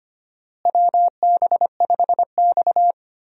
Morse code, heard as a keyed tone, about 25 wpm, 700 Hz. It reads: WB5X
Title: USA Call Signs - 4 Characters - Repeated 3x - 25wpm